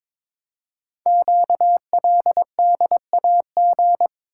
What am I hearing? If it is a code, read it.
Morse code, heard as a keyed tone, about 22 wpm, 700 Hz. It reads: QLDAG
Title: Code Groups - 5 Letters - 22wpm